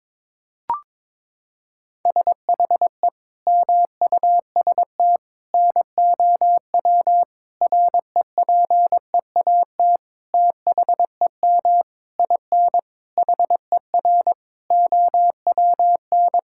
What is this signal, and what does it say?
Morse code, heard as a keyed tone, about 22 wpm, 700 Hz. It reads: SHE MUST NOW REPEAT THEM IN HER OWN
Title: Sentences from Top 1600 Words 22wpm